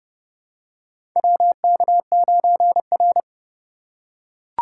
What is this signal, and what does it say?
Morse code, heard as a keyed tone, about 31 wpm, 700 Hz. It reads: WK9R
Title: USA Call Signs - 2x Character Spacing - 30wpm